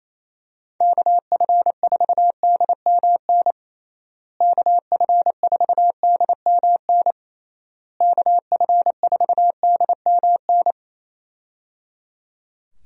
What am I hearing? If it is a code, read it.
Morse code, heard as a keyed tone, about 28 wpm, 700 Hz. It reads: KF4DMN KF4DMN KF4DMN
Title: USA Call Signs - 6 Characters - Repeated 3x - 28wpm